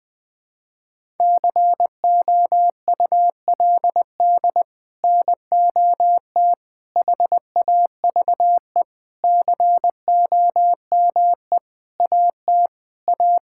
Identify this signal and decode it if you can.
Morse code, heard as a keyed tone, about 20 wpm, 700 Hz. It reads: COULD NOT HAVE COME AT A
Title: Sets of 8 Words - 20wpm